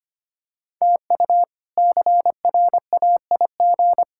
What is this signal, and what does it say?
Morse code, heard as a keyed tone, about 25 wpm, 700 Hz. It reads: TU CRAIG
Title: K1USN SST QSO Practice - 25wpm